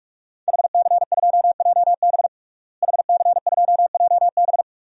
Morse code, heard as a keyed tone, about 46 wpm, 700 Hz. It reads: HC2JB HC2JB
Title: DXCC-100 Call Signs - 5 Characters - Repeated 2x - 45wpm